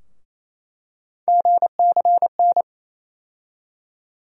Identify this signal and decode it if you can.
Morse code, heard as a keyed tone, about 28 wpm, 700 Hz. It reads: GCN
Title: USA Call Sign Postfixes - 3 Letters 28wpm